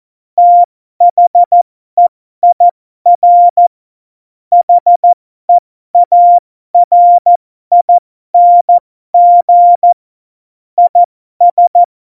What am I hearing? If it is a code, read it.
Morse code, heard as a keyed tone, about 14 wpm, spaced as for 12 wpm, 700 Hz. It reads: THEIR HEARING IS
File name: Morsekurs / Witze